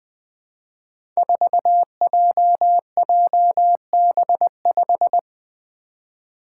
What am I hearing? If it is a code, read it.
Morse code, heard as a keyed tone, about 20 wpm, 700 Hz. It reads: 4JJB5
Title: Code Groups - 5 Letter:Numbers - 20wpm